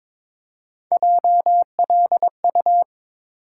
Morse code, heard as a keyed tone, about 22 wpm, 700 Hz. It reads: JLU
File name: practice / 22wpm / Three Letter-Number Combinations - 22wpm